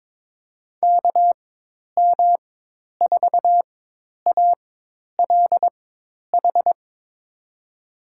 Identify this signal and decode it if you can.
Morse code, heard as a keyed tone, about 22 wpm, spaced as for 10 wpm, 700 Hz. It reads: KM4ALH
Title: USA Call Signs - 4x Character Spacing - 22wpm